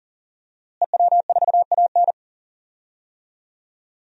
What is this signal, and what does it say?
Morse code, heard as a keyed tone, about 40 wpm, 700 Hz. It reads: EW4AN
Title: DXCC-100 Call Signs - 5 Characters - Repeated 3x - 40wpm